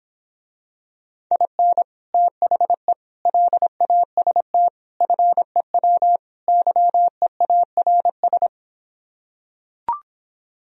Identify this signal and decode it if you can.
Morse code, heard as a keyed tone, about 26 wpm, 700 Hz. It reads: IN THE LAST FEW YEARS